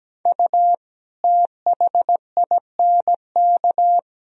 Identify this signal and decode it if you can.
Morse code, heard as a keyed tone, about 17 wpm, 700 Hz. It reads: U THINK
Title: Sentences from Top 200 Words and Abbreviations 17wpm